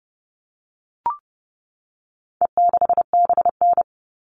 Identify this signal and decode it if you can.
Morse code, heard as a keyed tone, about 30 wpm, 700 Hz. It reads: E6BN
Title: All Cut Numbers - 4-Digit 30wpm